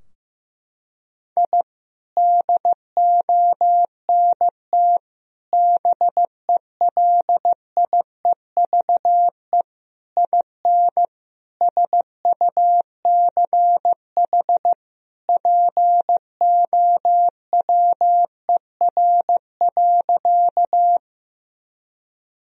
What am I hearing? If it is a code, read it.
Morse code, heard as a keyed tone, about 15 wpm, 700 Hz. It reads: I DONT BELIEVE IN SUCH POWER.
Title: Sentences from Top 1000 Words - Review Part 2 - 15wpm